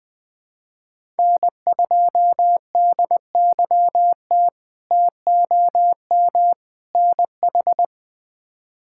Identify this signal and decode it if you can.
Morse code, heard as a keyed tone, about 20 wpm, 700 Hz. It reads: N2DYT TOM NH
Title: K1USN SST QSO Practice - 20wpm